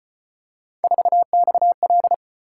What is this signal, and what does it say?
Morse code, heard as a keyed tone, about 33 wpm, 700 Hz. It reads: 4XL